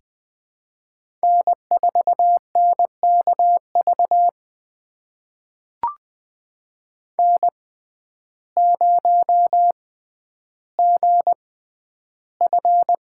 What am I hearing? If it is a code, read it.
Morse code, heard as a keyed tone, about 20 wpm, spaced as for 11 wpm, 700 Hz. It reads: N4NKV N0GF
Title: USA Call Signs - 5 Characters - 6x Character Spacing - 20wpm